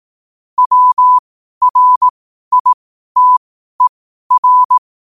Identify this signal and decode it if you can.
Morse code, heard as a keyed tone, about 18 wpm, spaced as for 12 wpm, 1000 Hz. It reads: WRITER
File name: qotdfiles / 12wpm